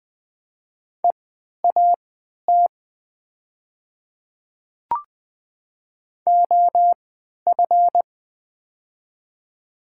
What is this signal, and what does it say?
Morse code, heard as a keyed tone, about 21 wpm, spaced as for 12 wpm, 700 Hz. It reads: EAT OF
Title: 3x Character Spacing - Top 500 Words - 20wpm